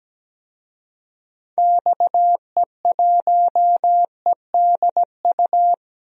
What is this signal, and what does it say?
Morse code, heard as a keyed tone, about 17 wpm, 700 Hz. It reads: XE1EDU
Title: DXCC-100 Call Signs - 6 Characters - Repeated 3x - 17wpm